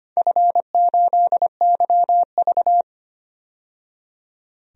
Morse code, heard as a keyed tone, about 25 wpm, 700 Hz. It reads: F8YV
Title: DXCC-100 Call Signs - 4 Characters - 25wpm